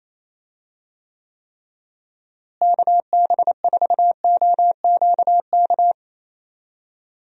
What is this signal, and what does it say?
Morse code, heard as a keyed tone, about 28 wpm, 700 Hz. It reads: KB4OQK
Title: USA Call Signs - 6 Characters - Repeated 3x - 28wpm